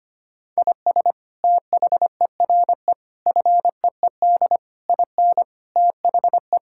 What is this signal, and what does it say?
Morse code, heard as a keyed tone, about 25 wpm, 700 Hz. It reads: IS THERE FEED IN THE
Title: Sentences from Top 1600 Words 25wpm